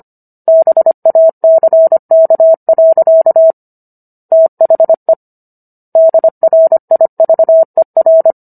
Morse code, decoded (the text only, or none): BACK. THE DRIVER